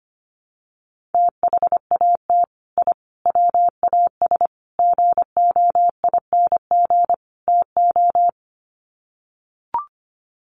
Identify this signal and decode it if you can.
Morse code, heard as a keyed tone, about 25 wpm, 700 Hz. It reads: THAT I WAS GOING TO
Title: Sets of 5 Words Encore 25wpm